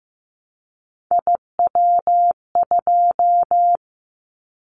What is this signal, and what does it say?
Morse code, heard as a keyed tone, about 15 wpm, 700 Hz. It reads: IW2
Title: Callsign Prefixes for Top 25 DXCC - 15wpm